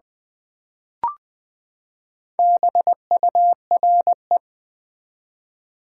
Morse code, heard as a keyed tone, about 20 wpm, 700 Hz. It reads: BURE
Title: Build A Word - 4 Letters At A Time - 11-Letter Words - 20wpm